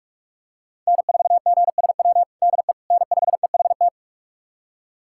Morse code, heard as a keyed tone, about 46 wpm, 700 Hz. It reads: N4GSW DE N5EHT
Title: USA Callsign DE Callsign - 5 Characters - 45wpm